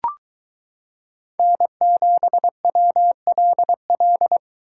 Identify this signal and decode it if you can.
Morse code, heard as a keyed tone, about 23 wpm, 700 Hz. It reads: N7WLL